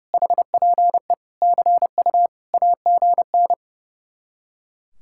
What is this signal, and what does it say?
Morse code, heard as a keyed tone, about 40 wpm, 700 Hz. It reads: HPE CU AGN